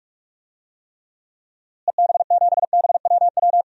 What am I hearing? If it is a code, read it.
Morse code, heard as a keyed tone, about 46 wpm, 700 Hz. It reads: EB7BWW
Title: DXCC-100 Call Signs - 6 Characters - Repeated 3x - 45wpm